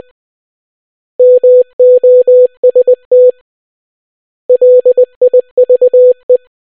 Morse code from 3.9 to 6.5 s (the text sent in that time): LIVE